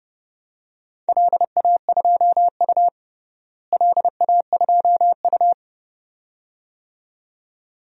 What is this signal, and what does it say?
Morse code, heard as a keyed tone, about 30 wpm, 700 Hz. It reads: LA2U LA2U
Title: DXCC-100 Call Signs - 4 Characters - Repeated 2x - 30wpm